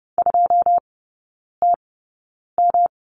Morse code, heard as a keyed tone, about 30 wpm, 700 Hz. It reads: 2 T M
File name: Lesson 19e - Letter w - Three Characters 30wpm